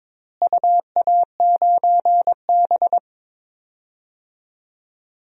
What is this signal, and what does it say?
Morse code, heard as a keyed tone, about 22 wpm, 700 Hz. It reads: UA9B
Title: DXCC-100 Call Signs - 4 Characters - Repeated 3x - 22wpm